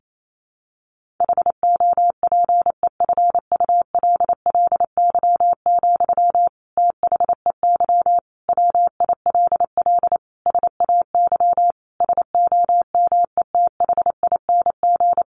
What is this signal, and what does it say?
Morse code, heard as a keyed tone, about 28 wpm, 700 Hz. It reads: HOPEFULLY, THEY WILL SAY SOMETHING
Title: Sentences from Top 1500 Words 28wpm